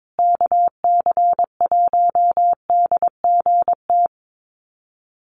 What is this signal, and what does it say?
Morse code, heard as a keyed tone, about 22 wpm, 690 Hz. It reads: KC1DGT
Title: USA Call Signs Repeated 3x 22wpm